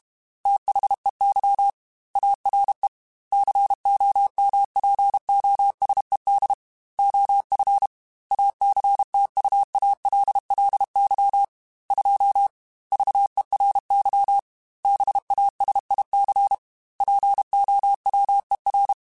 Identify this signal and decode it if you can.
Morse code, heard as a keyed tone, about 32 wpm, 800 Hz. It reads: THEY ARE COMPOSED OF ACTUALLY 2 VERY BASIC POWER
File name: Technical Subjects / Technical Subjects-mp3s / Sw Pwr Sup @30